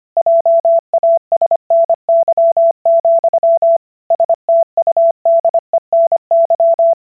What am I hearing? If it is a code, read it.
Morse code, heard as a keyed tone, about 25 wpm, 670 Hz. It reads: JASNY, STUDENY